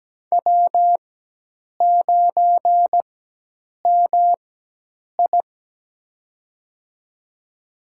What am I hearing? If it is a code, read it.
Morse code, heard as a keyed tone, about 17 wpm, spaced as for 8 wpm, 700 Hz. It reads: W9MI
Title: USA Call Signs - 4 Characters - 4x Character Spacing - 17wpm